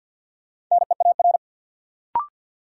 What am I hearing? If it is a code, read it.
Morse code, heard as a keyed tone, about 47 wpm, 700 Hz. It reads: NEAR
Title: Sentences from Top 200 Words - New Word Drill - 50wpm